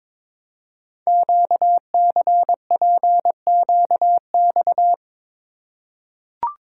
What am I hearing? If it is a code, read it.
Morse code, heard as a keyed tone, about 23 wpm, 700 Hz. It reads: QCPQX
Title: Code Groups - 5 Letters - 22wpm